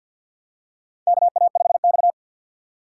Morse code, heard as a keyed tone, about 52 wpm, 700 Hz. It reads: KA5X
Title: USA Call Signs - 4 Characters - 3x Character Spacing - 50wpm